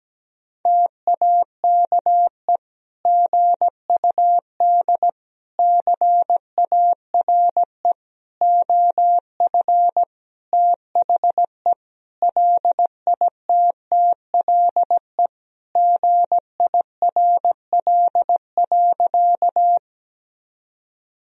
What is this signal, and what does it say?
Morse code, heard as a keyed tone, about 17 wpm, 700 Hz. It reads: TAKE GUD CARE OF THE LITTLE GIRL.
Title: Sentences from Top 400 Words and Abbreviations 17wpm